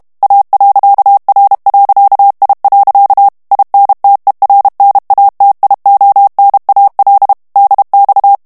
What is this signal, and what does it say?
Morse code, heard as a keyed tone, about 32 wpm, 800 Hz. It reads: A.R.I. INTERNATIONAL DX
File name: texte clair 2 vitesse 30 mots minute